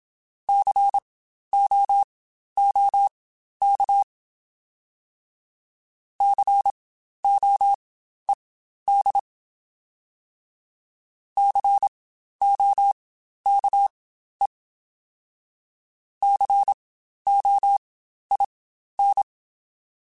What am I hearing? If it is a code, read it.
Morse code, heard as a keyed tone, about 27 wpm, spaced as for 12 wpm, 800 Hz. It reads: COOK COED COKE COIN